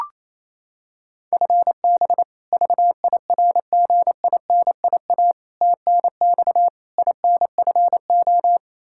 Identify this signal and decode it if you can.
Morse code, heard as a keyed tone, about 28 wpm, 700 Hz. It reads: FB VIRGINIA TNX INFO
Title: K1USN SST QSO Practice 28wpm